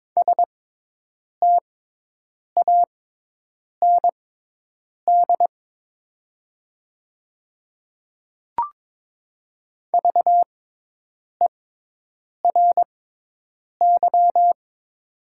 Morse code, heard as a keyed tone, about 22 wpm, spaced as for 8 wpm, 700 Hz. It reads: STAND VERY
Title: 6x Character Spacing - Top 300 Words - 22wpm